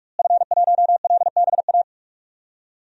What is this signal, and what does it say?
Morse code, heard as a keyed tone, about 46 wpm, 700 Hz. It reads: F1LBA
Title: DXCC-100 Call Signs - 5 Characters - Repeated 2x - 45wpm